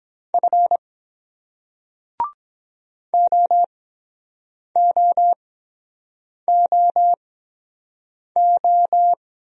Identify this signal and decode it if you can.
Morse code, heard as a keyed tone, about 22 wpm, spaced as for 7 wpm, 700 Hz. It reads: F OOOO